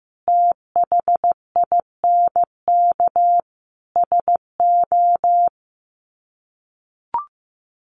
Morse code, heard as a keyed tone, about 15 wpm, 700 Hz. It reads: THINK SO
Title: Sets of 5 Words 15wpm